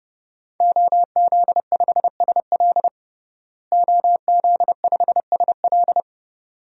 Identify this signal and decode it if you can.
Morse code, heard as a keyed tone, about 30 wpm, 700 Hz. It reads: OZ5SL OZ5SL
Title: DXCC-100 Call Signs - 5 Characters - Repeated 2x - 30wpm